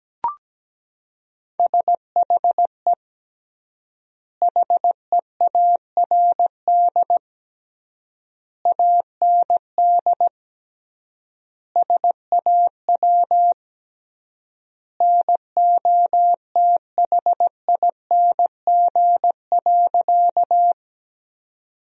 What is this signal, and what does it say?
Morse code, heard as a keyed tone, about 17 wpm, 700 Hz. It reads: SHE HEARD AND SAW NOTHING.
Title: 3x Word-Spacing - Sentences from Top 500 Words - 17wpm